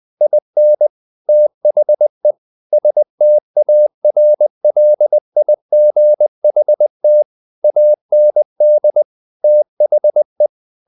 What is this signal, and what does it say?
Morse code, heard as a keyed tone, about 20 wpm, 600 Hz. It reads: IN THE STARLIGHT AND THE
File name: the old man and the sea part 2 20wpm